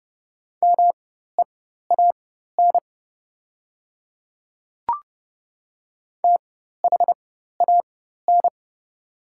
Morse code, heard as a keyed tone, about 30 wpm, spaced as for 14 wpm, 700 Hz. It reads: MEAN THAN
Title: 4x Character Spacing - Top 300 Words - 30wpm